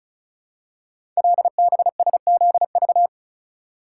Morse code, heard as a keyed tone, about 36 wpm, 700 Hz. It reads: LBSZV
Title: Code Groups - 5 Letters - 35wpm